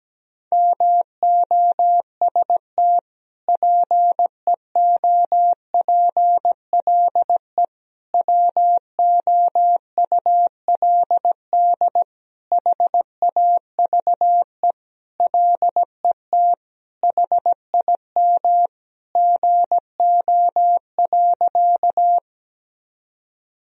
Morse code, heard as a keyed tone, about 17 wpm, 700 Hz. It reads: MOST PEOPLE WOULD HAVE LET HIM GO.
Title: Sentences from Top 1000 Words - Review Part 1 - 17wpm